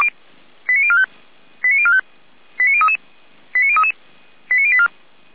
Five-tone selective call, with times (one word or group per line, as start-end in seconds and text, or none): 0.7-1.0 89045
1.6-2.0 89045
2.6-3.0 89033
3.5-3.9 89033
4.5-4.9 89084